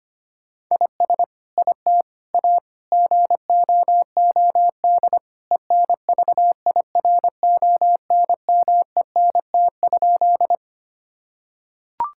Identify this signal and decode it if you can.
Morse code, heard as a keyed tone, about 25 wpm, 700 Hz. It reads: IS IT A GOOD ENVIRONMENT?
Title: Sentences from Top 1000 Words - Review Part 5 - 25wpm